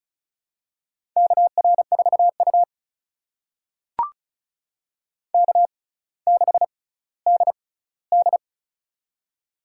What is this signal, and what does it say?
Morse code, heard as a keyed tone, about 36 wpm, spaced as for 18 wpm, 700 Hz. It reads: KR4U K6DD
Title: USA Call Signs - 4 Characters - 6x Character Spacing - 35wpm